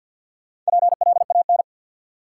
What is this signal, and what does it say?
Morse code, heard as a keyed tone, about 53 wpm, 700 Hz. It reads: PLAN
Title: Four Letter Words - 50wpm